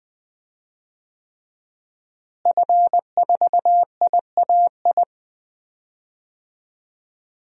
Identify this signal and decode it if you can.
Morse code, heard as a keyed tone, about 21 wpm, 700 Hz. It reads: F4IAI
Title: DXCC-100 Call Signs - 5 Characters - Repeated 3x - 20wpm